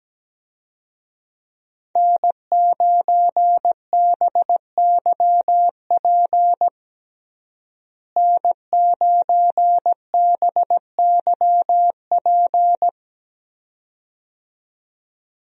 Morse code, heard as a keyed tone, about 17 wpm, 700 Hz. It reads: N9BYP N9BYP
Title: USA Call Signs - 5 Characters - Repeated 2x - 17wpm